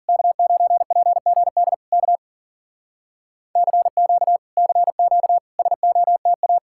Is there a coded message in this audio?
Yes: Morse code from 0.1 to 6.6 s, heard as a keyed tone, about 53 wpm, 700 Hz. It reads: K9PGD K CQ CQ SOTA